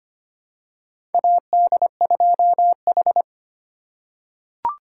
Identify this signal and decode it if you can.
Morse code, heard as a keyed tone, about 25 wpm, 700 Hz. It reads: AD2H
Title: USA Call Signs - 4 Characters - 3x Character Spacing - 25wpm